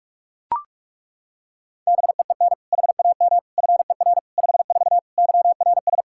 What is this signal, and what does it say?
Morse code, heard as a keyed tone, about 41 wpm, 700 Hz. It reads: BEEN HAM FER 54 YRS
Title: K1USN SST QSO Practice 45wpm